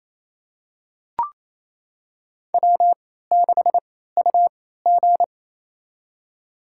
Morse code, heard as a keyed tone, about 28 wpm, spaced as for 16 wpm, 700 Hz. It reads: W6UG
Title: USA Call Signs - 4 Characters - 3x Character Spacing - 28wpm